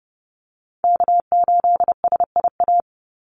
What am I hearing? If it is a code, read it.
Morse code, heard as a keyed tone, about 30 wpm, 700 Hz. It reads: K8SIA
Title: Call Signs Repeated 2x - 30wpm